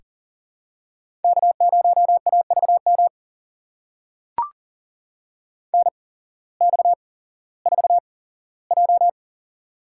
Morse code, heard as a keyed tone, about 40 wpm, spaced as for 19 wpm, 700 Hz. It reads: K0AVM NX4J
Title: USA Call Signs - 8x Character Spacing - 40wpm